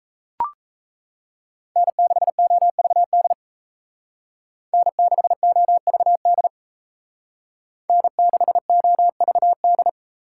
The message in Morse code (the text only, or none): N6OVD N6OVD N6OVD